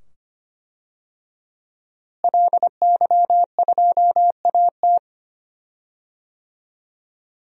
Morse code, heard as a keyed tone, about 25 wpm, 700 Hz. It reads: LY2AT
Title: DXCC-100 Call Signs - 5 Characters - Repeated 2x - 25wpm